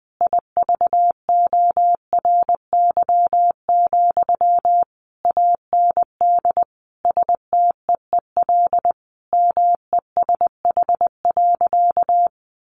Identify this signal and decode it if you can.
Morse code, heard as a keyed tone, about 19 wpm, 700 Hz. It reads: IVORY, AND STEEL MESH.
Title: A Sound Of Thunder 20wpm